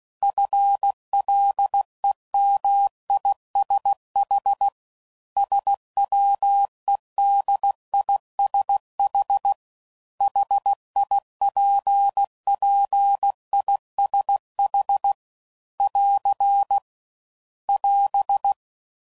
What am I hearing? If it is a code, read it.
Morse code, heard as a keyed tone, about 16 wpm, 800 Hz. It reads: FLEMISH SWEDISH HIPPISH <AR> <AS>